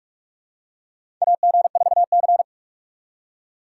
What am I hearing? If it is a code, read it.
Morse code, heard as a keyed tone, about 44 wpm, 700 Hz. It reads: AG4C
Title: USA Call Signs - 4 Characters - 6x Character Spacing - 45wpm